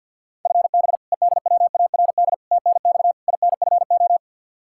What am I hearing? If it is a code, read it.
Morse code, heard as a keyed tone, about 47 wpm, 700 Hz. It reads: FB EDWARD TNX INFO
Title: Practice QSO - Exchange 3 - More Info - 50wpm